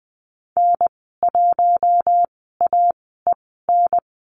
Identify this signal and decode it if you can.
Morse code, heard as a keyed tone, about 20 wpm, spaced as for 15 wpm, 700 Hz. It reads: N1AEN
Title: 048 Lesson 9g - Number 4 - Call Signs 20wpm